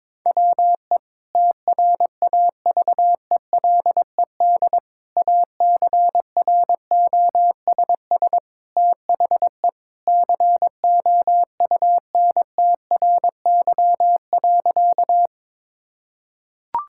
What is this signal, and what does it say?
Morse code, heard as a keyed tone, about 22 wpm, 700 Hz. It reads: WE TRAVELED ACROSS THE COUNTRY.
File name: practice / 22wpm / Sentences with 2-Word Phrases - 22wpm